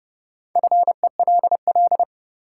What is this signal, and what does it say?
Morse code, heard as a keyed tone, about 30 wpm, 700 Hz. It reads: FELL
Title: Sentences from Top 1600 Words - New Word Drill - 30wpm